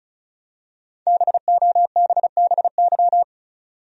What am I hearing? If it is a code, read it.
Morse code, heard as a keyed tone, about 35 wpm, 700 Hz. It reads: BOBBY
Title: USA Common Names - Max 5 Letters - 35wpm